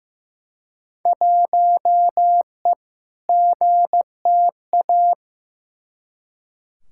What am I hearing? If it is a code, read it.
Morse code, heard as a keyed tone, about 15 wpm, 700 Hz. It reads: 1E GTA
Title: Field Day Exchanges - 15wpm